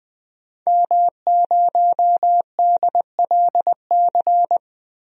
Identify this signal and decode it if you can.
Morse code, heard as a keyed tone, about 20 wpm, 700 Hz. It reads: M0DLC
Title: DXCC-100 Call Signs - 5 Characters - 20wpm